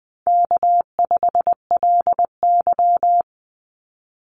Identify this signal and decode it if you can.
Morse code, heard as a keyed tone, about 20 wpm, 700 Hz. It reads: K5LY
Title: Call Signs Repeated 2x - 20wpm